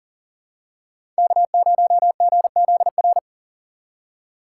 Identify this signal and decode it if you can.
Morse code, heard as a keyed tone, about 40 wpm, 700 Hz. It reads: K0GZR
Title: USA Call Signs - 6x Character Spacing - 40wpm